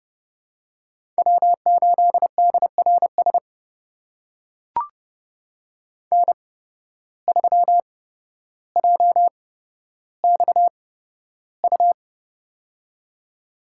Morse code, heard as a keyed tone, about 30 wpm, spaced as for 13 wpm, 700 Hz. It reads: W8DRS N3JXU
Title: USA Call Signs - 5 Characters - 8x Character Spacing - 30wpm